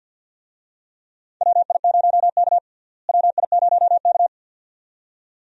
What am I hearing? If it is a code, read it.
Morse code, heard as a keyed tone, about 52 wpm, 700 Hz. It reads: WI0K WI0K
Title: USA Call Signs - 4 Characters - Repeated 2x - 50wpm